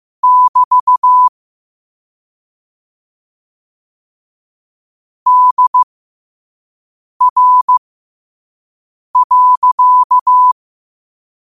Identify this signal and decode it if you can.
Morse code, heard as a keyed tone, about 15 wpm, spaced as for 5 wpm, 1000 Hz. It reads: <BT> DR.
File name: 2024-10-07-5wpm